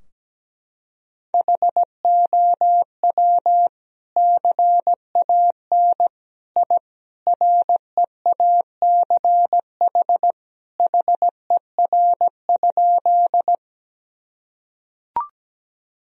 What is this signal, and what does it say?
Morse code, heard as a keyed tone, about 17 wpm, 700 Hz. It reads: HOW CAN I REACH HER?
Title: Sentences from Top 1000 Words - Review Part 2 - 17wpm